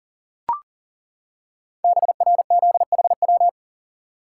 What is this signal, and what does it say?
Morse code, heard as a keyed tone, about 38 wpm, 690 Hz. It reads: BRZHW
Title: Code Groups - 5 Letters - 40wpm